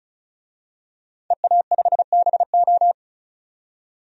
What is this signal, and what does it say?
Morse code, heard as a keyed tone, about 34 wpm, 700 Hz. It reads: EA5BO
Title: DXCC-100 Call Signs - 5 Characters - 35wpm